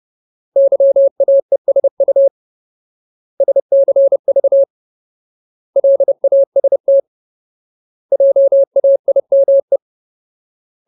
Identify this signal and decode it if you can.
Morse code, heard as a keyed tone, about 30 wpm, 550 Hz. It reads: YAESU SCV LAST JAIME
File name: Ultimate Word List / 30WPM Ultimate